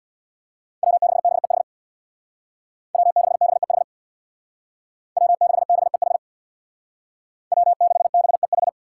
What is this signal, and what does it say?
Morse code, heard as a keyed tone, about 62 wpm, 700 Hz. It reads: W6BEH W6BEH W6BEH W6BEH
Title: Speed Racing - USA 1x3 Call Signs 50wpm